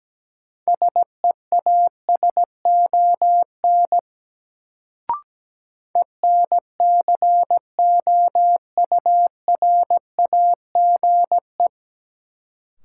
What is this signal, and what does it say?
Morse code, heard as a keyed tone, about 17 wpm, 700 Hz. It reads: SEASON ENCOURAGE
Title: Sentences from Top 1400 Words - New Word Drill Alt - 17wpm